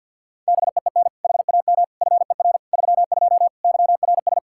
Morse code, heard as a keyed tone, about 50 wpm, 700 Hz. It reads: BEEN HAM FER 32 YRS
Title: K1USN SST QSO Practice 50wpm